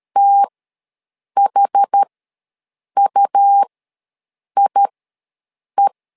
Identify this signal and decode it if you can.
Morse code, heard as a keyed tone, about 13 wpm, spaced as for 7 wpm, 800 Hz. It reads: THUIE
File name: LEZIONI / 2 UHDN / 2 UH ws 12wpm 1730